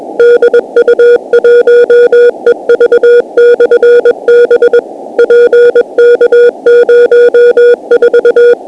Morse code, heard as a keyed tone, about 21 wpm, 480 Hz. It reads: DU1EV/B PK04
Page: DU1EV/b Beacon 50.008 MHz Operated by Eddie DU1EV and located in PK04, QUEZON CITY, PHILIPPINES